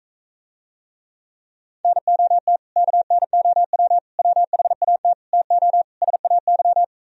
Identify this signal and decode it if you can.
Morse code, heard as a keyed tone, about 45 wpm, 700 Hz. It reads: NOT KNOW WHAT TO SAY